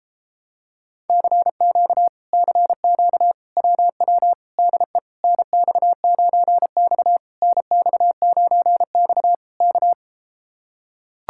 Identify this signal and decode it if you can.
Morse code, heard as a keyed tone, about 33 wpm, 700 Hz. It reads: CQ CQ WW DE NX9X NX9X K